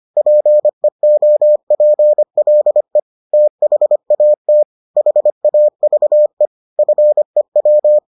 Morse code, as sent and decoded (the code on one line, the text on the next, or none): .--. --- .--. .-.. / - .... .- - / .... .- ...- . / ..-. .--
PEOPLE THAT HAVE FEW